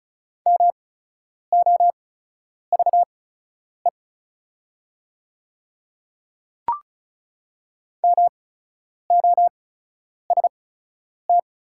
Morse code, heard as a keyed tone, about 36 wpm, spaced as for 10 wpm, 690 Hz. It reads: MOVE MOST
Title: 8x Character Spacing - Top 200 Words - 35wpm